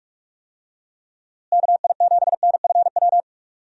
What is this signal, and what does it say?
Morse code, heard as a keyed tone, about 46 wpm, 700 Hz. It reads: KI7NFW
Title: USA Call Signs - 6 Characters - Repeated 2x - 45wpm